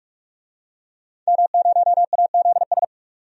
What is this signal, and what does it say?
Morse code, heard as a keyed tone, about 46 wpm, 700 Hz. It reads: M0AZS
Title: DXCC-100 Call Signs - 5 Characters - 45wpm